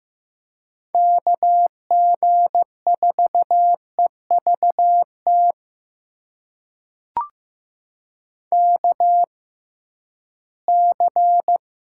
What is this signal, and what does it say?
Morse code, heard as a keyed tone, about 15 wpm, spaced as for 11 wpm, 700 Hz. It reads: KG4EVT KC
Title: USA Call Signs - 6 Characters - 6x Character Spacing - 15wpm